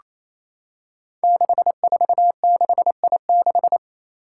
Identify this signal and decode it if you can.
Morse code, heard as a keyed tone, about 29 wpm, 700 Hz. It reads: 646I6
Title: Five Challenging Letter-Number Combinations 28wpm